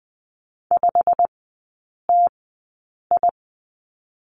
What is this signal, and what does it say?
Morse code, heard as a keyed tone, about 20 wpm, spaced as for 8 wpm, 700 Hz. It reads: 5TI
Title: 087 Lesson 15e - Number 5 - Three Characters 20wpm